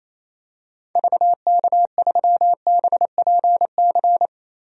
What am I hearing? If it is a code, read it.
Morse code, heard as a keyed tone, about 28 wpm, 700 Hz. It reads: VK3BPC
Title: DXCC-100 Call Signs - 6 Characters - Repeated 3x - 28wpm